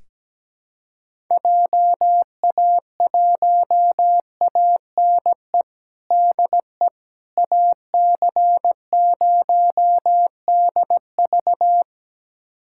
Morse code, heard as a keyed tone, about 17 wpm, 700 Hz. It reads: JA1ANE DE AC0DV
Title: USA Callsign DE Callsign 17wpm